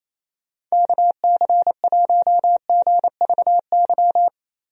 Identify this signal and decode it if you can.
Morse code, heard as a keyed tone, about 28 wpm, 700 Hz. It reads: KC1GVY
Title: USA Call Signs - 6 Characters - Repeated 2x - 28wpm